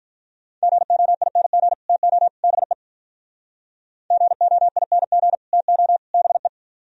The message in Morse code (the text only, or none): GOING TO BE GOING TO BE